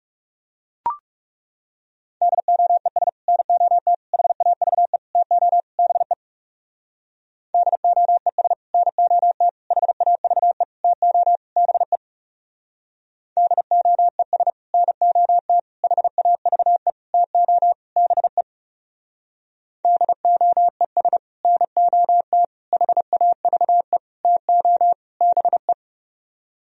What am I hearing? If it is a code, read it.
Morse code, heard as a keyed tone, about 36 wpm, 700 Hz. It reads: DOES NOT HAVE TO BE DOES NOT HAVE TO BE DOES NOT HAVE TO BE DOES NOT HAVE TO BE
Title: Speed-Racing - Sets of 5 Words 30wpm